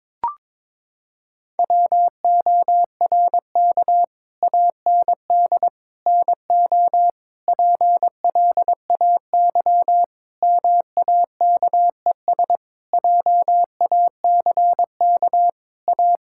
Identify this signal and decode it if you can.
Morse code, heard as a keyed tone, about 22 wpm, 700 Hz. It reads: WORK AND NO PLAY MAKES JACK A
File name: Sentences with 7-Word Phrases - 22wpm